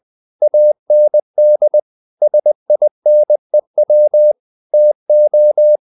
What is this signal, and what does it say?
Morse code, heard as a keyed tone, about 20 wpm, 600 Hz. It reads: AND SINEW TO
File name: CW / Practice / If20